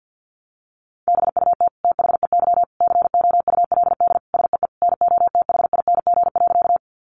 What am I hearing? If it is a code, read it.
Morse code, heard as a keyed tone, about 50 wpm, 700 Hz. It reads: BUT THEY COULD SEE NOTHING.
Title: Sentences from Top 500 Words 50wpm